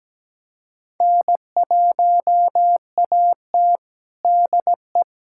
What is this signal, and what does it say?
Morse code, heard as a keyed tone, about 17 wpm, 700 Hz. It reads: N1AT DE
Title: USA Callsign DE Callsign - 4 Characters - 17wpm